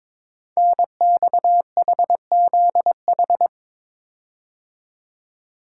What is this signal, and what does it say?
Morse code, heard as a keyed tone, about 22 wpm, 700 Hz. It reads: NXHZH
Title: Code Groups - 5 Letters - 22wpm